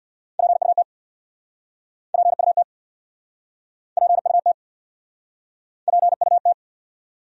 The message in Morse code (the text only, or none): PUT PUT PUT PUT